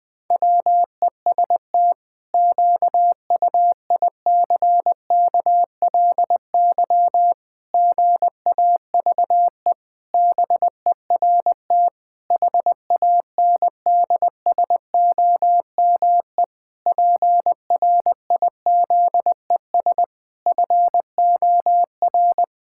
Morse code, heard as a keyed tone, about 19 wpm, 700 Hz. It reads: WEST QUICKLY GAVE BERT HANDSOME PRIZES FOR
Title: Pangram Challenge v2 - 20wpm